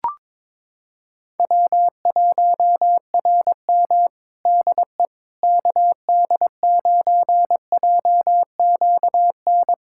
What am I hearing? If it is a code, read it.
Morse code, heard as a keyed tone, about 22 wpm, 700 Hz. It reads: W1RM DE KD9JQN
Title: K1USN SST QSO Practice 22wpm